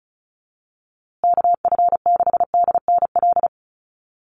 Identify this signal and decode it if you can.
Morse code, heard as a keyed tone, about 34 wpm, 700 Hz. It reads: KF6DNL
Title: USA Call Signs Repeated 3x 35wpm